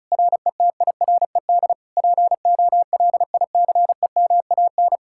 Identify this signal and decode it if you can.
Morse code, heard as a keyed tone, about 48 wpm, spaced as for 42 wpm, 700 Hz. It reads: RETIRED POLICEMAN
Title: Speed-Racing Common QSO Phrases Encore - 35wpm